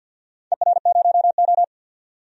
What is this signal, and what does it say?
Morse code, heard as a keyed tone, about 50 wpm, 700 Hz. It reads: ER0O
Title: DXCC-100 Call Signs - 4 Characters - 50wpm